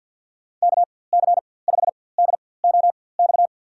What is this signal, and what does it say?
Morse code, heard as a keyed tone, about 52 wpm, spaced as for 23 wpm, 700 Hz. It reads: KC5DOX
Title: USA Call Signs - 6 Characters - 4x Character Spacing - 50wpm